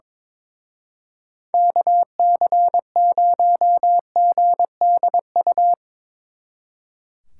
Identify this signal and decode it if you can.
Morse code, heard as a keyed tone, about 25 wpm, 700 Hz. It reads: KC0GDU
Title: Speed Racing - USA 2x3 Call Signs 22wpm